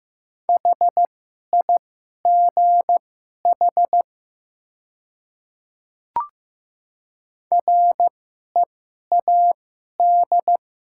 Morse code, heard as a keyed tone, about 15 wpm, spaced as for 11 wpm, 700 Hz. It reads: HIGH READ
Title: 2x Character Spacing - Top 300 Words - 15wpm